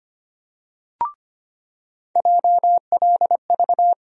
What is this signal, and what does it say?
Morse code, heard as a keyed tone, about 25 wpm, 700 Hz. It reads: JLV
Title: USA Call Sign Postfixes - 3 Letters 25wpm